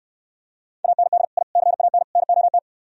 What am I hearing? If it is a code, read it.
Morse code, heard as a keyed tone, about 68 wpm, 700 Hz. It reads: AND I CAN NOT